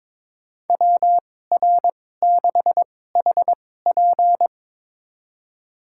Morse code, heard as a keyed tone, about 22 wpm, spaced as for 16 wpm, 700 Hz. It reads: WR6HP
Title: USA Call Signs - 5 Characters - 2x Character Spacing - 22wpm